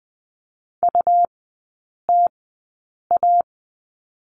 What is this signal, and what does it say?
Morse code, heard as a keyed tone, about 19 wpm, spaced as for 8 wpm, 700 Hz. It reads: UTA
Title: 133 Lesson 22e - Question Mark - Three Characters 20wpm